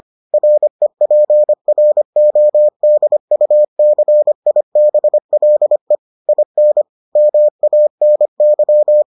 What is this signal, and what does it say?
Morse code, heard as a keyed tone, about 25 wpm, 600 Hz. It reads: REPRODUCIBLE IN MANY